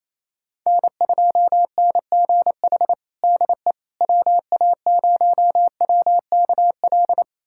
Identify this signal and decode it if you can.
Morse code, heard as a keyed tone, about 28 wpm, 700 Hz. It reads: N2NGH DE WA0WKL
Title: USA Callsign DE Callsign 28wpm